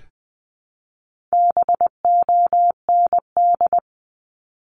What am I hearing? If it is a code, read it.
Morse code, heard as a keyed tone, about 20 wpm, 700 Hz. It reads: BOND
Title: 260 Lesson 42e - Words 20-20wpm